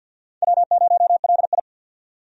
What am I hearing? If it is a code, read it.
Morse code, heard as a keyed tone, about 51 wpm, 700 Hz. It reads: W0LI
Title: USA Call Signs - 4 Characters - 50wpm